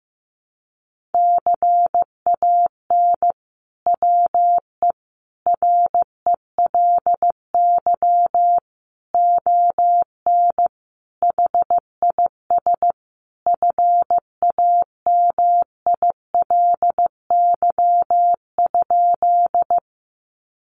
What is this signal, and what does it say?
Morse code, heard as a keyed tone, about 15 wpm, 700 Hz. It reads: CAN WE RELY ON HIS FAMILY?
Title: Sentences from Top 1200 Words 15wpm